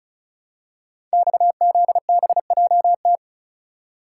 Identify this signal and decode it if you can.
Morse code, heard as a keyed tone, about 36 wpm, 700 Hz. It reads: XZBJT
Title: Code Groups - 5 Letter:Numbers - 35wpm